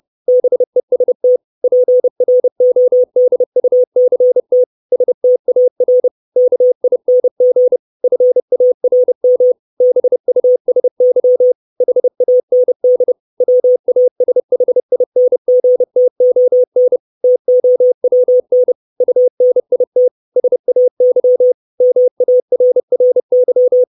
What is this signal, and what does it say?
Morse code, heard as a keyed tone, about 30 wpm, 500 Hz. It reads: BEST PRODUCT STAR KING FARM BUSY HAND WASHINGTON TOWN UNIT SAY MARRY
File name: 632 Words 30WPM